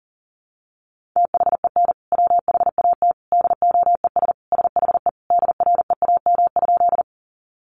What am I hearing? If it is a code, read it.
Morse code, heard as a keyed tone, about 40 wpm, 700 Hz. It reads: THEN WHAT DOES SHE DREAM?
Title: Sentences from Top 1500 Words 40wpm